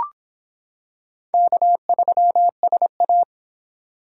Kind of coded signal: Morse code

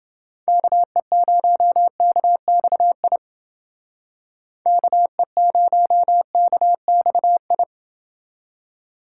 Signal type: Morse code

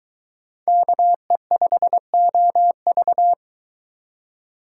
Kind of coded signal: Morse code

